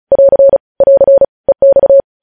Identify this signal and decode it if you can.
Morse code, heard as a keyed tone, about 36 wpm, 550 Hz. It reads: <AR> <AR> EX
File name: podcast / News headlines-1x-35x35